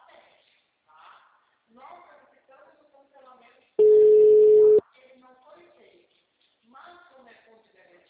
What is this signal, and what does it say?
Ringback tone (Europe, 425 Hz), with speech.